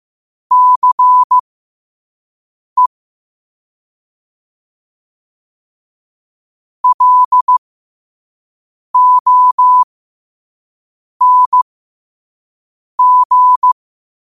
Morse code, heard as a keyed tone, about 15 wpm, spaced as for 5 wpm, 1000 Hz. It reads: CE LONG
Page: New quotes every day in morse code at 5 Words per minute.